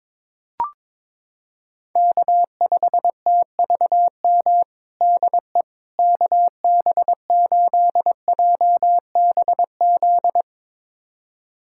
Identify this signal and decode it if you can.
Morse code, heard as a keyed tone, about 22 wpm, 700 Hz. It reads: K5TVM DE KB8JBZ
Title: USA Callsign DE Callsign 22wpm